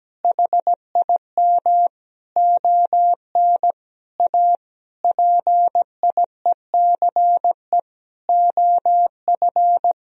Morse code, heard as a keyed tone, about 17 wpm, 700 Hz. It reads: HIM ON A PIECE OF
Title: Sentences from Top 1000 Words - Review Part 2 - 17wpm